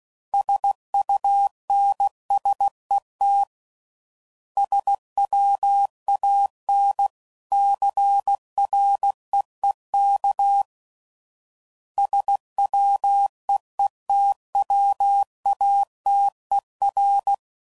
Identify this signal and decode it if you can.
Morse code, heard as a keyed tone, about 16 wpm, 800 Hz. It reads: SUNSET SWAN CREEK SWEETWATER
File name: CW Code Groups @15WPM / fromSM's / Cities